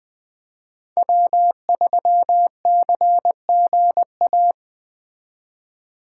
Morse code, heard as a keyed tone, about 20 wpm, 700 Hz. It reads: W3CGA